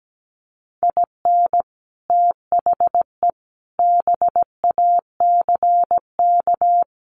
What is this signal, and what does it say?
Morse code, heard as a keyed tone, about 17 wpm, 700 Hz. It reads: IN THE BACK
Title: Sets of 5 Words Encore2 17wpm